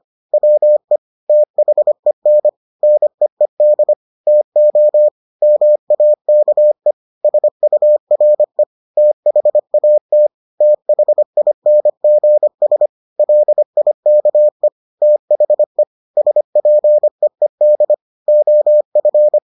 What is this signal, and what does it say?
Morse code, heard as a keyed tone, about 25 wpm, 600 Hz. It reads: WE THEN NEED TO MAKE SURE THAT THINGS LIKE THE SPEED OF